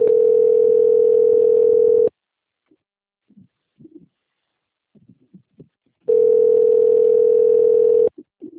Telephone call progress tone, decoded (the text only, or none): ringback tone